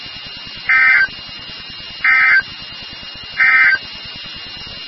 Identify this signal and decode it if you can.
Emergency Alert System (SAME) tones, heard as harsh buzzing bursts. It is heard from 0.6 to 3.9 s.
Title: NOAA Weather Radio WXK36